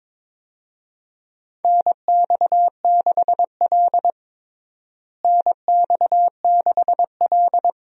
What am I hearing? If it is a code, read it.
Morse code, heard as a keyed tone, about 23 wpm, 700 Hz. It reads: NX6L NX6L
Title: USA Call Signs - 4 Characters - Repeated 2x - 22wpm